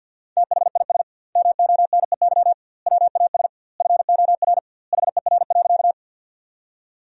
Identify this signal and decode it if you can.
Morse code, heard as a keyed tone, about 50 wpm, 700 Hz. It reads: THIS MONEY WAS FOR HER.
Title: 2x Word-Spacing - Sentences from Top 400 Words - 50wpm